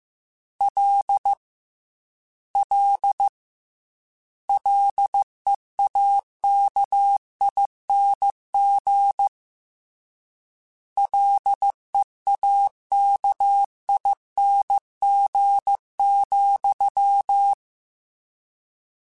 Morse code, heard as a keyed tone, about 15 wpm, 800 Hz. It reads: L L LEAKING LEAKING,
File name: Combinations / Combinations-mp3s / Combinations@13